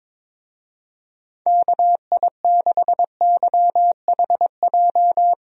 Morse code, heard as a keyed tone, about 22 wpm, 700 Hz. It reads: KI6YHJ
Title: USA Call Signs - 6 Characters - Repeated 2x - 22wpm